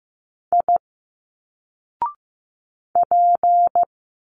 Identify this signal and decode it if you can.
Morse code, heard as a keyed tone, about 15 wpm, 700 Hz. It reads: I P
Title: Single Letters 15wpm